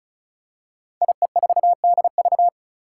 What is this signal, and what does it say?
Morse code, heard as a keyed tone, about 36 wpm, 700 Hz. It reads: IE4DV
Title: Five Challenging Letter-Number Combinations 35wpm